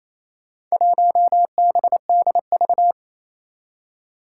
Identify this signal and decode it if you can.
Morse code, heard as a keyed tone, about 28 wpm, 700 Hz. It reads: 1BDV
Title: Four Challenging Letter-Number Combinations 28wpm